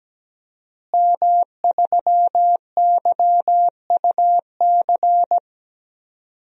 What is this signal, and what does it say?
Morse code, heard as a keyed tone, about 17 wpm, 700 Hz. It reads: M3YUC
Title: DXCC-100 Call Signs - 5 Characters - Repeated 3x - 17wpm